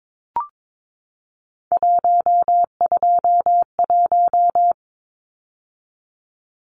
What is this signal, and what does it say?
Morse code, heard as a keyed tone, about 22 wpm, 700 Hz. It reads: 121
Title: Numbers - 3-Digit 22wpm